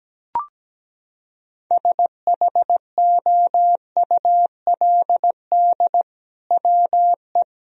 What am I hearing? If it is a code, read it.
Morse code, heard as a keyed tone, about 17 wpm, 700 Hz. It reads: SHOULD WE
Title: Sentences from Top 1000 Words - Review Part 3 - 17wpm